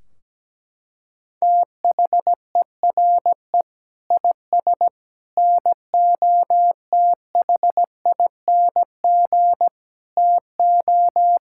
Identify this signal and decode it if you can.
Morse code, heard as a keyed tone, about 17 wpm, 700 Hz. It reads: THERE IS NOTHING TO
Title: Sentences from Top 1600 Words 17wpm